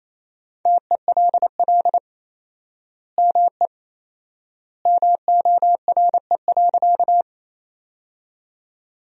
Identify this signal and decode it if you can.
Morse code, heard as a keyed tone, about 28 wpm, 700 Hz. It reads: TELL ME MORE.
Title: 4x Word-Spacing - Sentences from Top 300 Words - 28wpm